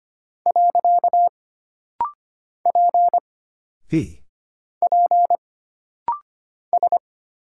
Morse code, heard as a keyed tone, about 25 wpm, 700 Hz. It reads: . P P S
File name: Single Character (Letter:Number:Punctuation:Prosign) - Rapid-Fire - 25wpm